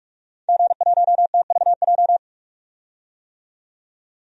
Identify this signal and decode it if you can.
Morse code, heard as a keyed tone, about 46 wpm, 700 Hz. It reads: G1TVJ
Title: DXCC-100 Call Signs - 5 Characters - 45wpm